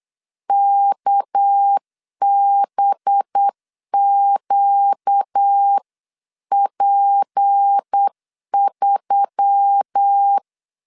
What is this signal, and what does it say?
Morse code, heard as a keyed tone, about 8 wpm, 800 Hz. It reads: KBQP3
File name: CW / DET / 1 6 1500C S B 8wpm 5142